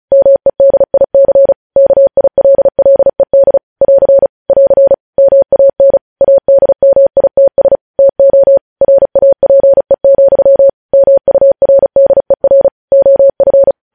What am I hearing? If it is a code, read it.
Morse code, heard as a keyed tone, about 36 wpm, 550 Hz. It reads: MEDIC KILLED <AR> <AR> MAN ADMITS TO RAPE, MURDER OF